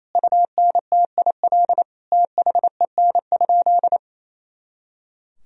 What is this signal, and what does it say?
Morse code, heard as a keyed tone, about 28 wpm, 700 Hz. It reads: UNTIL THEN?
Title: Sentences from Top 500 Words and Abbreviations 28wpm